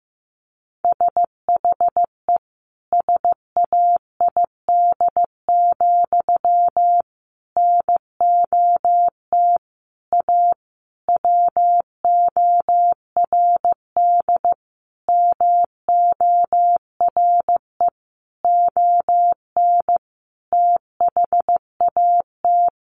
Morse code, heard as a keyed tone, about 15 wpm, 700 Hz. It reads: SHE SAID, NOT A WORD MORE ON THAT
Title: Sentences from Top 800 Words 15wpm